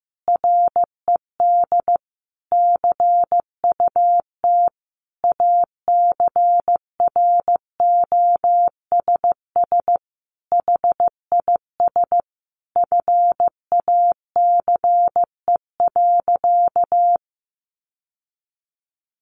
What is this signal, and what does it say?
Morse code, heard as a keyed tone, about 15 wpm, 700 Hz. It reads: RED CUT ACROSS HIS FACE.
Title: Sentences from Top 700 Words 15wpm